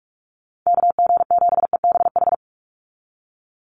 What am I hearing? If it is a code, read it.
Morse code, heard as a keyed tone, about 44 wpm, 700 Hz. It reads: KG7EBH
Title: USA Call Signs 45wpm